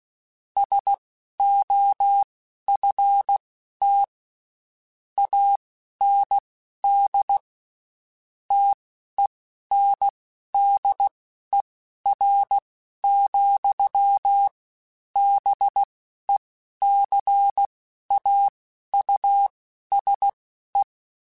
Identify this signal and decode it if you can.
Morse code, heard as a keyed tone, about 16 wpm, spaced as for 11 wpm, 800 Hz. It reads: SOFT AND TENDER, BECAUSE
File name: Spaced / Chuckles Spaced-mp3s / SoftWords sp @15